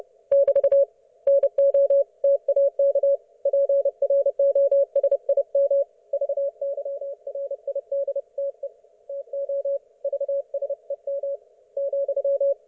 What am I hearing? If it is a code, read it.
Morse code, heard as a keyed tone, about 30 wpm, 560 Hz. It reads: <BT> NOTAK PROSIM VYRIDTE TO VSEM ,
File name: Archiv / CUC179 / ok1fhi cul179